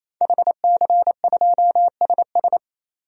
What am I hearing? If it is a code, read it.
Morse code, heard as a keyed tone, about 28 wpm, 700 Hz. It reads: HC2SS
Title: DXCC-100 Call Signs - 5 Characters - 28wpm